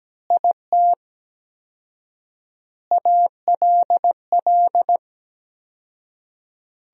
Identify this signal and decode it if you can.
Morse code, heard as a keyed tone, about 17 wpm, 700 Hz. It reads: IT ALL
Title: 4x Word-Spacing - Sentences from Top 500 Words - 17wpm